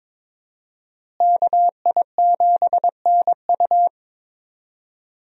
Morse code, heard as a keyed tone, about 21 wpm, 700 Hz. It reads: KI7NU
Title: USA Call Signs - 5 Characters - Repeated 3x - 22wpm